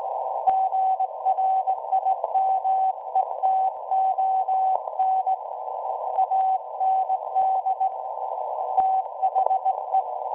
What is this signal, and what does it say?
Morse code, heard as a keyed tone, about 18 wpm, 760 Hz. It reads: GRIMETON AND THE
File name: WebSDR records / SAQ at PI4THT part od 1005z 17-2kHz